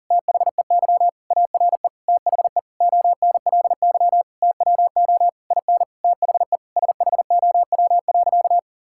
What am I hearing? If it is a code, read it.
Morse code, heard as a keyed tone, about 40 wpm, 700 Hz. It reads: THEY ARE THE ONLY TWO IN THE SHOW.
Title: Sentences from Top 1000 Words - Review Part 6 - 40wpm